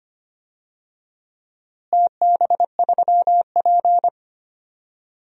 Morse code, heard as a keyed tone, about 25 wpm, 700 Hz. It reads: TB3P
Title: DXCC-100 Call Signs - 4 Characters - Repeated 3x - 25wpm